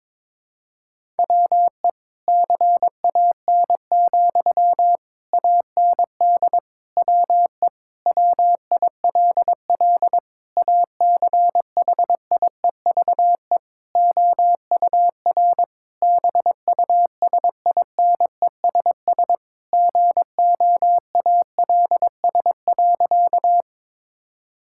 Morse code, heard as a keyed tone, about 22 wpm, 700 Hz. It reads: WE CAN, AND WE WILL ACHIEVE OUR BUSINESS GOALS.
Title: Sentences from Top 1000 Words - Review Part 6 - 22wpm